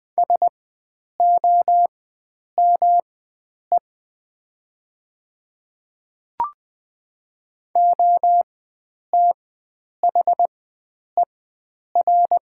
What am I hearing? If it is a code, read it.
Morse code, heard as a keyed tone, about 20 wpm, spaced as for 9 wpm, 700 Hz. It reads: SOME OTHER
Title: 4x Character Spacing - Top 100 Words - 20wpm